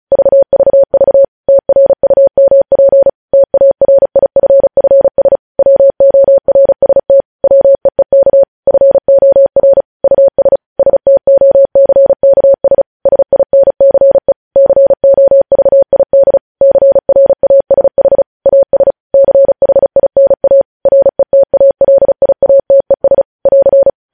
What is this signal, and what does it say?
Morse code, heard as a keyed tone, about 36 wpm, 550 Hz. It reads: VVV TRUMP TARIFFS WORST WEEK FOR US STOCKS SINCE COVID CRASH AS CHINA RETALIATES <AR>
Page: news headlines in morse code at 35 wpm